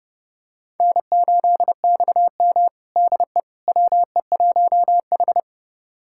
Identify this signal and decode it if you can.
Morse code, heard as a keyed tone, about 30 wpm, 700 Hz. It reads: N8XM DE WE1H
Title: USA Callsign DE Callsign - 4 Characters - 30wpm